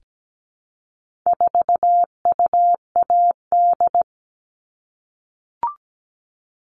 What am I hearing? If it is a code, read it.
Morse code, heard as a keyed tone, about 17 wpm, 700 Hz. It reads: 4UAD
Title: All Cut Numbers - 4-Digit 17wpm